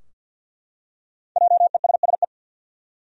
Morse code, heard as a keyed tone, about 52 wpm, 700 Hz. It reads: JESSE
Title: USA Common Names - 50wpm